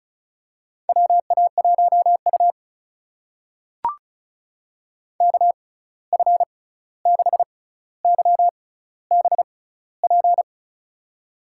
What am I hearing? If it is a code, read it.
Morse code, heard as a keyed tone, about 36 wpm, spaced as for 16 wpm, 700 Hz. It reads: WA1U KF6YBP
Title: USA Call Signs - 6x Character Spacing - 35wpm